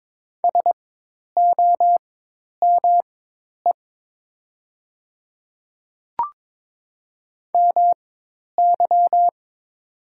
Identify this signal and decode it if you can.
Morse code, heard as a keyed tone, about 22 wpm, spaced as for 10 wpm, 700 Hz. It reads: SOME MY
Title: 4x Character Spacing - Top 200 Words - 22wpm